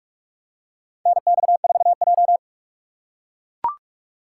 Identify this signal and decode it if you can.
Morse code, heard as a keyed tone, about 44 wpm, 700 Hz. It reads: NX4J
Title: USA Call Signs - 4x Character Spacing - 45wpm